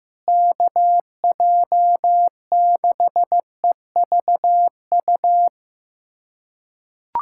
Morse code, heard as a keyed tone, about 15 wpm, 700 Hz. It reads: KJ6EVU
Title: USA Callsign DE Callsign - 6 Characters - 15wpm